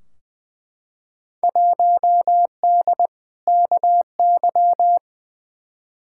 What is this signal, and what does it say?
Morse code, heard as a keyed tone, about 20 wpm, 700 Hz. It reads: 1D KY
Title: Field Day Exchanges - 20wpm